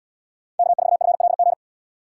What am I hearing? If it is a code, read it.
Morse code, heard as a keyed tone, about 71 wpm, 700 Hz. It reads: K4FLG